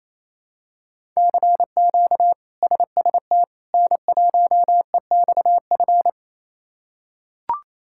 Morse code, heard as a keyed tone, about 28 wpm, 700 Hz. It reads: CQ SST N1EXF
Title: K1USN SST QSO Practice - 28wpm